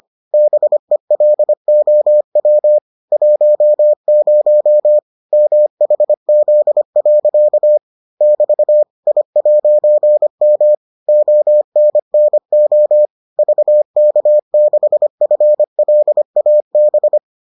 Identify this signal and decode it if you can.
Morse code, heard as a keyed tone, about 25 wpm, 600 Hz. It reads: BELOW 10 MHZ. <BT> I'M ONNO VK6FLAB